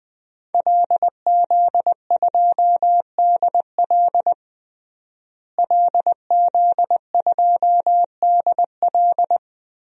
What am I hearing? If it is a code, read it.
Morse code, heard as a keyed tone, about 20 wpm, 700 Hz. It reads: LZ2DL LZ2DL
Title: DXCC-100 Call Signs - 5 Characters - Repeated 3x - 20wpm